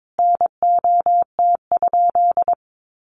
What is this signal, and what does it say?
Morse code, heard as a keyed tone, about 22 wpm, 700 Hz. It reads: NOT?
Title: Sentences from Top 600 Words 22wpm